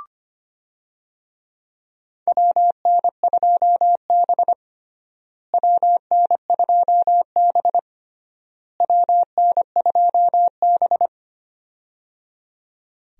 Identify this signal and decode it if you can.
Morse code, heard as a keyed tone, about 25 wpm, 700 Hz. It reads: WN2B WN2B WN2B
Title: USA Call Signs - 4 Characters - Repeated 3x - 25wpm